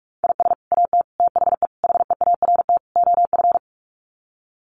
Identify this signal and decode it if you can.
Morse code, heard as a keyed tone, about 44 wpm, 700 Hz. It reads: IS AT THE HEART OF
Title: Sets of 5 Words Encore2 45wpm